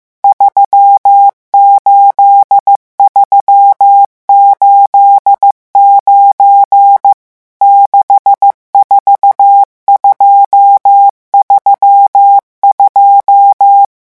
Morse code, heard as a keyed tone, about 15 wpm, 800 Hz. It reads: 38389 64232
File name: morse mp3 / revision11 chiffres vitesse 14